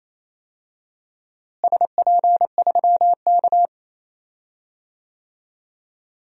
Morse code, heard as a keyed tone, about 28 wpm, 700 Hz. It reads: SP3K
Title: DXCC-100 Call Signs - 4 Characters - Repeated 3x - 28wpm